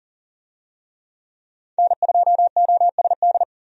Morse code, heard as a keyed tone, about 40 wpm, 700 Hz. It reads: N2OSD
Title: USA Call Signs - 5 Characters - 40wpm